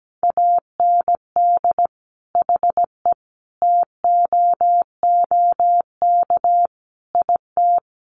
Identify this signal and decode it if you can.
Morse code, heard as a keyed tone, about 17 wpm, 700 Hz. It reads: AND HE TOOK IT
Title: Sentences from Top 1400 Words 17wpm